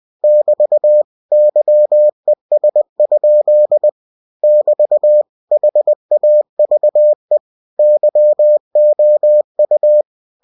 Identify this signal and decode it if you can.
Morse code, heard as a keyed tone, about 20 wpm, 600 Hz. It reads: <BT> YES? <BT> HAVE YOU
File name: cw-books / winnie-the-pooh / winnie-20wpm-page-0007